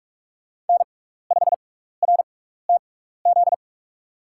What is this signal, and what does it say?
Morse code, heard as a keyed tone, about 44 wpm, spaced as for 15 wpm, 700 Hz. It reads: N5RTZ
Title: USA Call Signs - 5 Characters - 6x Character Spacing - 45wpm